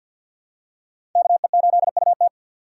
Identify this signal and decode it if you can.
Morse code, heard as a keyed tone, about 47 wpm, 690 Hz. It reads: KE8UT
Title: USA Call Signs - 4x Character Spacing - 50wpm